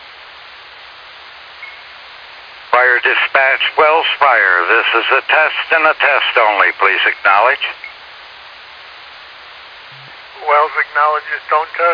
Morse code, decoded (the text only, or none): none